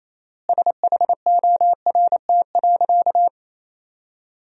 Morse code, heard as a keyed tone, about 27 wpm, 700 Hz. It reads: SHORT.
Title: 4x Word-Spacing - Sentences from Top 500 Words - 28wpm